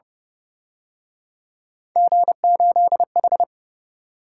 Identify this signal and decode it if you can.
Morse code, heard as a keyed tone, about 30 wpm, 700 Hz. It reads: G8H
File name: Speed-Racing - Three Letter-Number Combinations - 25wpm